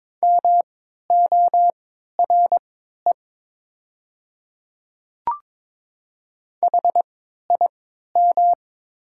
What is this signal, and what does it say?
Morse code, heard as a keyed tone, about 22 wpm, spaced as for 12 wpm, 690 Hz. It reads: MORE HIM
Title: 3x Character Spacing - Top 100 Words - 22wpm